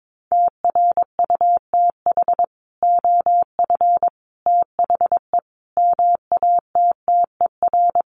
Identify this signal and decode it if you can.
Morse code, heard as a keyed tone, about 22 wpm, 700 Hz. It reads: TRUTH OF THE MATTER
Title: Sets of 5 Words Encore2 22wpm